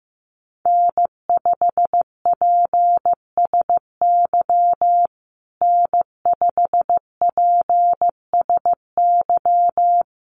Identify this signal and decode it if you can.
Morse code, heard as a keyed tone, about 15 wpm, 700 Hz. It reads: N5PSY N5PSY
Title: USA Call Signs Repeated 2x 15wpm